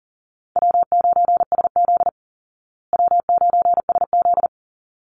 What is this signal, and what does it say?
Morse code, heard as a keyed tone, about 37 wpm, 700 Hz. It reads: W9SZ W9SZ
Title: Call Signs Repeated 3x - 40wpm